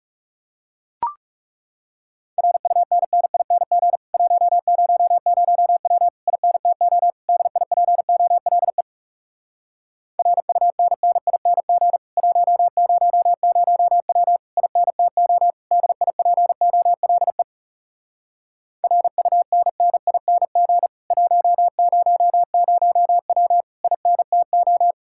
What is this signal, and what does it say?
Morse code, heard as a keyed tone, about 40 wpm, 700 Hz. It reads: RUNNING 100W INTO DIPOLE RUNNING 100W INTO DIPOLE RUNNING 100W INTO